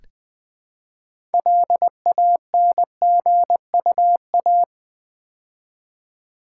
Morse code, heard as a keyed tone, about 20 wpm, 700 Hz. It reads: LANGUA
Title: Build A Word - 3 Letters At A Time - 8-Letter Words - 20wpm